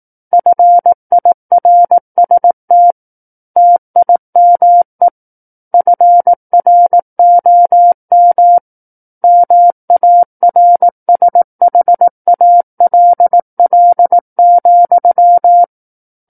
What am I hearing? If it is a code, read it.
Morse code, heard as a keyed tone, about 18 wpm, 700 Hz. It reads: FIRST TIME FROM MARSHALL,
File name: Book One / 09-Chapter 9 - 18 WPM